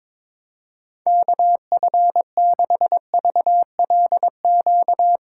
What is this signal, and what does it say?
Morse code, heard as a keyed tone, about 22 wpm, 700 Hz. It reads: KF6VLQ
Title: USA Call Signs - 6 Characters - 2x Character Spacing - 22wpm